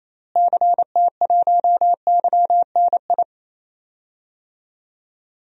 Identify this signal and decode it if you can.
Morse code, heard as a keyed tone, about 28 wpm, 700 Hz. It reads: CT1YNI
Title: DXCC-100 Call Signs - 6 Characters - Repeated 2x - 28wpm